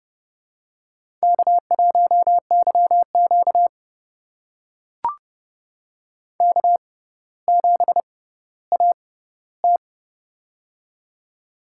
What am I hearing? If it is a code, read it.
Morse code, heard as a keyed tone, about 30 wpm, spaced as for 15 wpm, 700 Hz. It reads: K1YQ K7AT
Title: USA Call Signs - 4 Characters - 6x Character Spacing - 30wpm